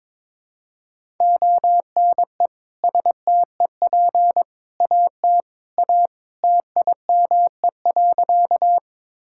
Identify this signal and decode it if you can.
Morse code, heard as a keyed tone, about 22 wpm, 700 Hz. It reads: ONE STEP AT A TIME.
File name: practice / 22wpm / Sentences with 3-Word Phrases - 22wpm